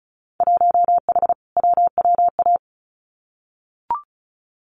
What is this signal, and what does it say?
Morse code, heard as a keyed tone, about 34 wpm, 700 Hz. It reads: 1H WWA
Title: Winter Field Day Exchanges 35wpm